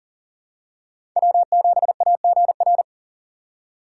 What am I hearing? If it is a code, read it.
Morse code, heard as a keyed tone, about 40 wpm, 700 Hz. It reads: W7AGR
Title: USA Call Signs - 5 Characters - 3x Character Spacing - 40wpm